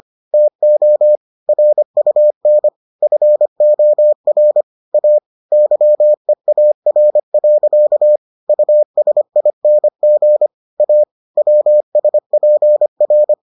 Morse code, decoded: TO RUN FOR A YEAR. USING A WSPR

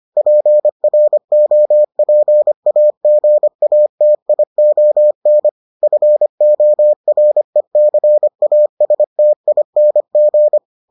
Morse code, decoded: PROPAGATION FORECASTING

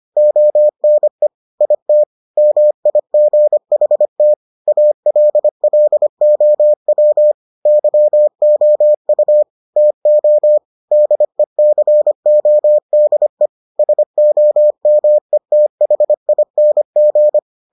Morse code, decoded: ONE IT MIGHT ALLOW YOU TO DECODE SOMETHING

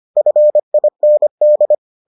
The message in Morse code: FIND

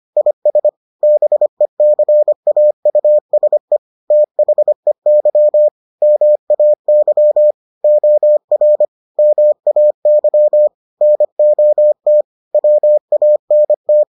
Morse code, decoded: IS BECAUSE THEY MAY OR MAY NOT WANT

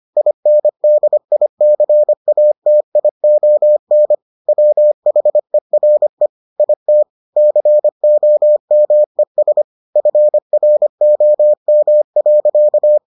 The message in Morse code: INDICATION WHERE IT COMES FROM.

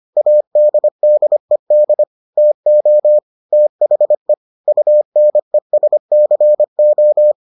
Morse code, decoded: ADDED TO THE UNESCO

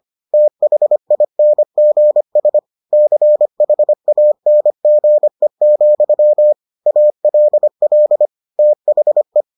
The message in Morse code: THINGS CHANGE, ALL THE

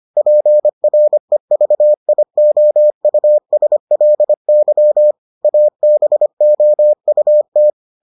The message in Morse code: PREVIOUSLY ABOUT

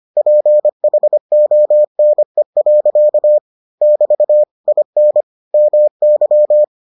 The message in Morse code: PHONE. <BT> IN MY

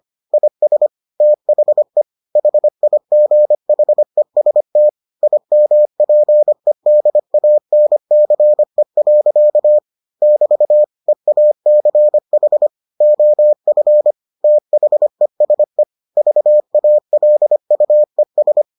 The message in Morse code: IS THE HIGHEST IMPEDANCE. <BT> EACH OF THESE VALUES

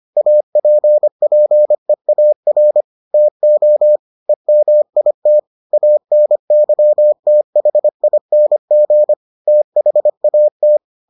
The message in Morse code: APPEAR TO EMIT ANYTHING THAT

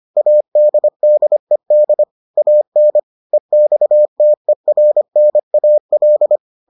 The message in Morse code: ADDED AN EXTERNAL

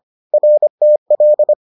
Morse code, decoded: RTL

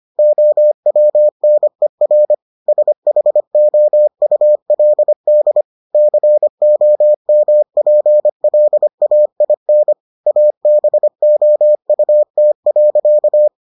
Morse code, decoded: OWNER SHOULD COMPLAIN ABOUT.

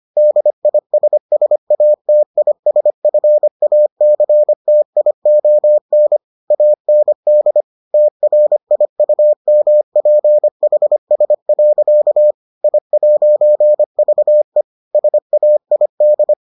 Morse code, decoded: DISSATISFACTION AND TRIUMPHS. I'VE SAID

